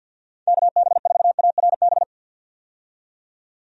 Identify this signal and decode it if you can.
Morse code, heard as a keyed tone, about 51 wpm, 700 Hz. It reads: KB4ARB
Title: USA Call Signs - 6 Characters - 50wpm